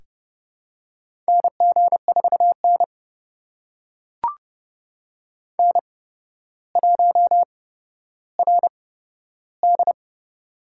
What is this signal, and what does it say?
Morse code, heard as a keyed tone, about 32 wpm, spaced as for 13 wpm, 700 Hz. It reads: NG4N N1RD
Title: USA Call Signs - 4 Characters - 8x Character Spacing - 30wpm